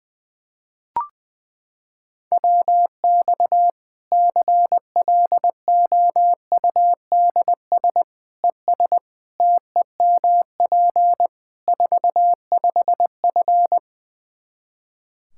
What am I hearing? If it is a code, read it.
Morse code, heard as a keyed tone, about 20 wpm, 700 Hz. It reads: WX CLOUDS ES TEMP 45F
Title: Practice QSO - Exchange 2 - Info - 20wpm